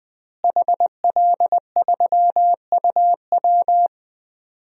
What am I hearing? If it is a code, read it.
Morse code, heard as a keyed tone, about 20 wpm, 700 Hz. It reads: HL3UW
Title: DXCC-100 Call Signs - 5 Characters - Repeated 2x - 20wpm